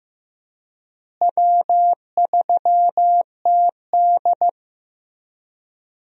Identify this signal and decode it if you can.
Morse code, heard as a keyed tone, about 15 wpm, 700 Hz. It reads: W3TD
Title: USA Call Signs - 4 Characters - Repeated 3x - 15wpm